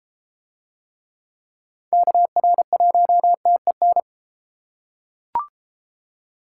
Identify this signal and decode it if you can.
Morse code, heard as a keyed tone, about 32 wpm, 700 Hz. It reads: KR1TEN